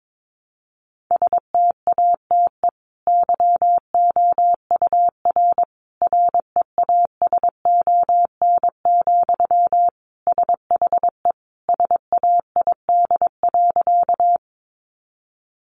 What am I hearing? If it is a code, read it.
Morse code, heard as a keyed tone, about 22 wpm, 700 Hz. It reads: STATE YOUR REASON, SHE SAID.
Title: Sentences from Top 700 Words 22wpm